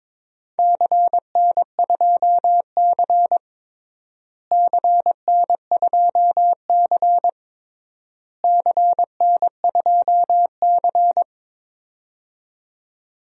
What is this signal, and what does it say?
Morse code, heard as a keyed tone, about 22 wpm, 700 Hz. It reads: CN2C CN2C CN2C
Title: DXCC-100 Call Signs - 4 Characters - Repeated 3x - 22wpm